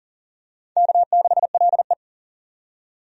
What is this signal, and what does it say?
Morse code, heard as a keyed tone, about 40 wpm, 700 Hz. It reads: K6LE
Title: USA Call Signs - 4 Characters - 40wpm